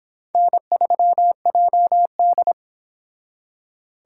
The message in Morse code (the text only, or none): N3JD